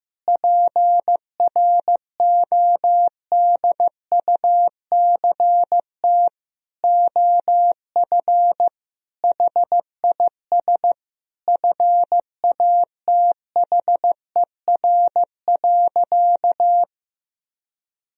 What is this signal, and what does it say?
Morse code, heard as a keyed tone, about 16 wpm, 700 Hz. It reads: PRODUCT OF HIS FATHER.
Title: Sentences from Top 1000 Words - Review Part 2 - 15wpm